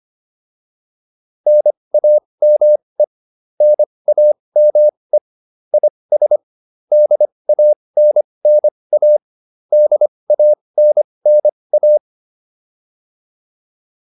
Morse code, heard as a keyed tone, about 25 wpm, spaced as for 20 wpm, 600 Hz. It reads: NAME NAME IS DANNA DANNA
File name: QSO406 20